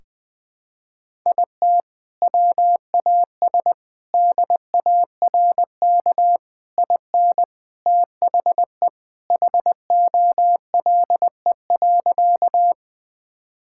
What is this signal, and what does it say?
Morse code, heard as a keyed tone, about 20 wpm, 700 Hz. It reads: IT WAS DARK IN THE HOLE.
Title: Sentences from Top 1600 Words 20wpm